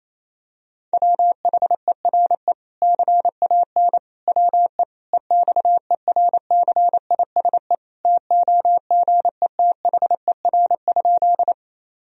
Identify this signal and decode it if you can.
Morse code, heard as a keyed tone, about 28 wpm, 700 Hz. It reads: WHERE CAN WE EXERCISE TOGETHER?
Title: Sentences from Top 1000 Words - Review Part 4 - 28wpm